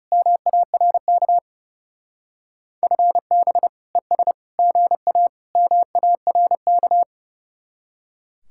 Morse code, heard as a keyed tone, about 37 wpm, 700 Hz. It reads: MARK FB ES GA MARK
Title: Speed-Racing Common QSO Phrases - 30wpm